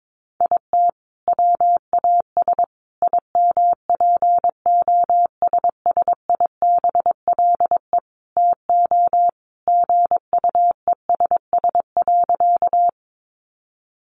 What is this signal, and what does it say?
Morse code, heard as a keyed tone, about 22 wpm, 700 Hz. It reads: IT WAS IMPOSSIBLE TO GUESS.
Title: Sentences from Top 1500 Words 22wpm